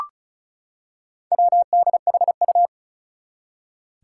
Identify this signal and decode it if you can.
Morse code, heard as a keyed tone, about 34 wpm, 700 Hz. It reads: WDHU
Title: Four Challenging Letter-Number Combinations 35wpm